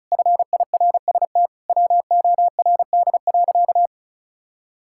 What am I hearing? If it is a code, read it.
Morse code, heard as a keyed tone, about 36 wpm, 700 Hz. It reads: FIRST WORD.
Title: Pangram Challenge v2 - 35wpm